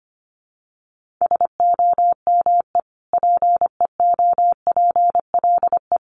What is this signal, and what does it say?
Morse code, heard as a keyed tone, about 25 wpm, 700 Hz. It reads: SOME PEOPLE
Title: Sentences from Top 1500 Words 25wpm